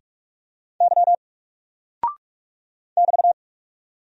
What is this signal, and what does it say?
Morse code, heard as a keyed tone, about 44 wpm, 700 Hz. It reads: Y <BT>
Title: Single Character (Letter:Number:Punctuation:Prosign) 45wpm